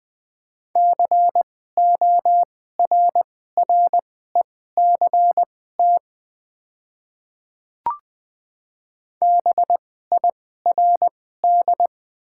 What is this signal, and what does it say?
Morse code, heard as a keyed tone, about 20 wpm, spaced as for 14 wpm, 700 Hz. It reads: CORRECT BIRD
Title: 2x Character Spacing - Top 500 Words - 20wpm